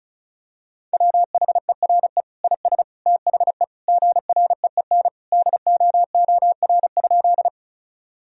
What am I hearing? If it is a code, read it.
Morse code, heard as a keyed tone, about 36 wpm, 700 Hz. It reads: WHERE IS THE GREEN DOOR?
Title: Sentences from Top 1000 Words - Review Part 2 - 35wpm